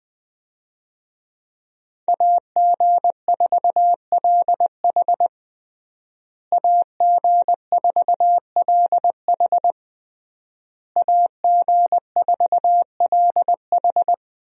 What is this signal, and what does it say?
Morse code, heard as a keyed tone, about 20 wpm, 700 Hz. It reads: AG4LH AG4LH AG4LH
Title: USA Call Signs - 5 Characters - Repeated 3x - 20wpm